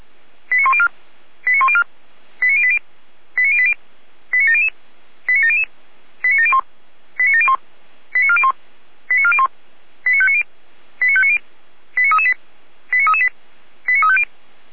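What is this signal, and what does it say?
Five-tone selective call (ZVEI), heard as quick runs of five tones, tones of about 70 ms. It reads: 89104 89104 89080 89080 89700 89700 89701 89701 89401 89401 89590 89590 89338 89338 89350
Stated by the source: Five-tone ZVEI selective call received on SDR.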